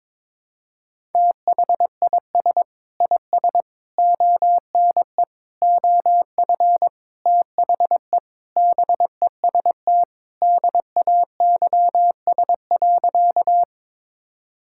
Morse code, heard as a keyed tone, about 22 wpm, 700 Hz. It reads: THIS IS ONE OF THE BEST DAYS.
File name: practice / 22wpm / Sentences with 4-Word Phrases - 22wpm